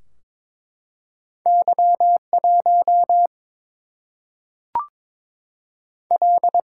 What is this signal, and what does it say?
Morse code, heard as a keyed tone, about 22 wpm, 700 Hz. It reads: Y 1 L
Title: Single-Letter Single-Number 22wpm